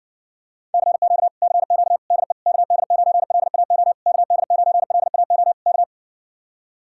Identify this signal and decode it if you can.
Morse code, heard as a keyed tone, about 69 wpm, 700 Hz. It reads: CQ CQ DE KD9LAO KD9LAO K